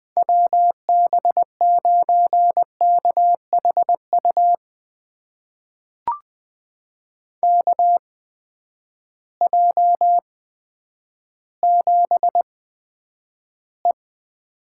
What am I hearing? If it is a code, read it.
Morse code, heard as a keyed tone, about 20 wpm, spaced as for 10 wpm, 700 Hz. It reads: WB9KHU KJ7E
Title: USA Call Signs - 6 Characters - 8x Character Spacing - 20wpm